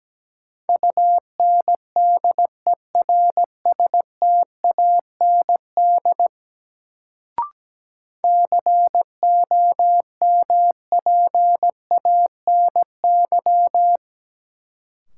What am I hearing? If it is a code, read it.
Morse code, heard as a keyed tone, about 17 wpm, 700 Hz. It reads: UNDERSTAND COMPANY
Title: Sentences from Top 600 Words - New Word Drill - 17wpm